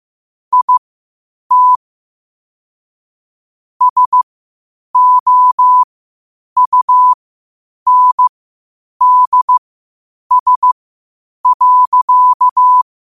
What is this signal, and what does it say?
Morse code, heard as a keyed tone, about 15 wpm, spaced as for 8 wpm, 1000 Hz. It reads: IT SOUNDS.
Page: Quotes for Sat, 16 Aug 2025 in Morse Code at 8 words per minute.